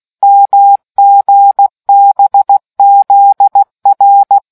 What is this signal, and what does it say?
Morse code, heard as a keyed tone, about 16 wpm, 800 Hz. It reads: MGBZR